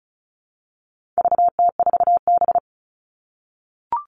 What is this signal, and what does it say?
Morse code, heard as a keyed tone, about 34 wpm, 700 Hz. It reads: VT4B
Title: All Cut Numbers - 4-Digit 35wpm